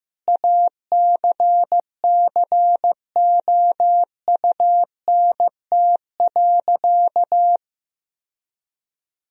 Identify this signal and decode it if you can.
Morse code, heard as a keyed tone, about 15 wpm, 700 Hz. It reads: ACCOUNT.
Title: Sentences from Top 1000 Words - Review Part 4 - 15wpm